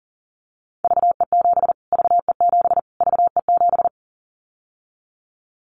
Morse code, heard as a keyed tone, about 40 wpm, 700 Hz. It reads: VE7 VE7 VE7
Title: Callsign Prefixes for Top 25 DXCC - Repeated 3x - 40wpm